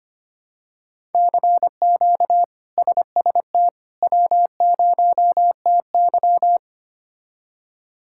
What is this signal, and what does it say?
Morse code, heard as a keyed tone, about 25 wpm, 700 Hz. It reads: CQ SST W0TY
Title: K1USN SST QSO Practice - 25wpm